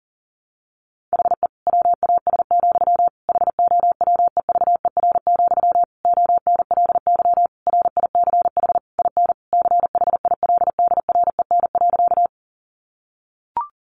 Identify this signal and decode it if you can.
Morse code, heard as a keyed tone, about 40 wpm, 700 Hz. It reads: HE WAS, HOWEVER, ONLY RICH IN CHILDREN.
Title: Sentences from Top 1100 Words 40wpm